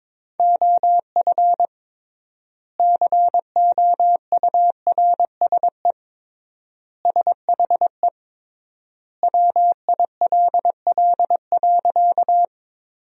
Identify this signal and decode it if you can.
Morse code, heard as a keyed tone, about 22 wpm, 700 Hz. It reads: OF COURSE SHE WILL.
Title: 3x Word-Spacing - Sentences from Top 500 Words - 22wpm